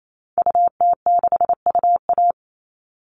Morse code, heard as a keyed tone, about 28 wpm, 700 Hz. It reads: UT6UA
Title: Call Signs Repeated 2x - 28wpm